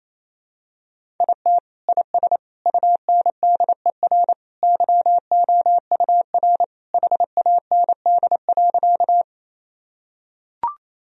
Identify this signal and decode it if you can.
Morse code, heard as a keyed tone, about 28 wpm, 700 Hz. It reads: IT IS UNDER YOUR HAND.
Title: Sentences from Top 1000 Words - Review Part 1 - 28wpm